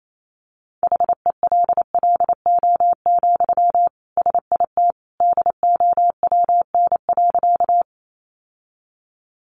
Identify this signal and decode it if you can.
Morse code, heard as a keyed tone, about 28 wpm, 700 Hz. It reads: HELLO, SIT DOWN.
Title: Sentences from Top 1100 Words 28wpm